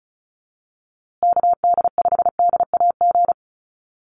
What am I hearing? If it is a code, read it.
Morse code, heard as a keyed tone, about 34 wpm, 700 Hz. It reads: KD5DAG
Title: USA Call Signs Repeated 2x 35wpm